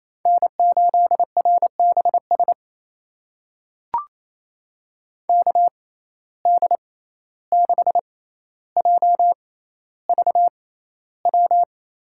Morse code, heard as a keyed tone, about 28 wpm, spaced as for 14 wpm, 700 Hz. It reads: N8RBS KD6JVW
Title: USA Call Signs - 6x Character Spacing - 28wpm